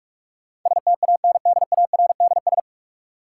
Morse code, heard as a keyed tone, about 46 wpm, 700 Hz. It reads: STANDARDS
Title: Sentences from Top 1000 Words - Review Part 6 - New Word Drill - 45wpm